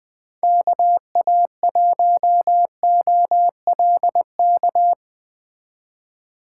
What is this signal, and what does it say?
Morse code, heard as a keyed tone, about 20 wpm, 700 Hz. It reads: KA1OLK
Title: Speed Racing - USA 2x3 Call Signs 17wpm